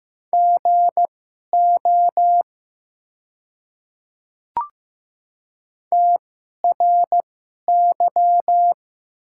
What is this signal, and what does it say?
Morse code, heard as a keyed tone, about 15 wpm, spaced as for 11 wpm, 700 Hz. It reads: GO TRY
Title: 2x Character Spacing - Top 200 Words - 15wpm